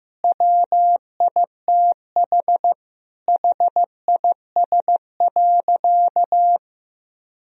Morse code, heard as a keyed tone, about 15 wpm, 700 Hz. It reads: WITH HIS.
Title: Sentences from Top 1000 Words - Review Part 3 - 15wpm